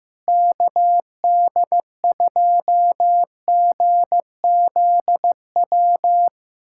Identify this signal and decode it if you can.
Morse code, heard as a keyed tone, about 15 wpm, 700 Hz. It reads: KD2GZW
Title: USA Callsign DE Callsign - 6 Characters - 15wpm